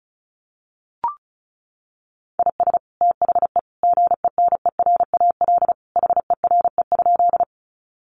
Morse code, heard as a keyed tone, about 34 wpm, 700 Hz. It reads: IS THE GENERAL HERE?
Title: Sentences from Top 600 Words 35wpm